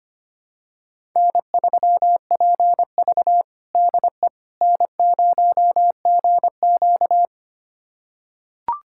Morse code, heard as a keyed tone, about 25 wpm, 700 Hz. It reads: N3PV DE N0GQ
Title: USA Callsign DE Callsign - 4 Characters - 25wpm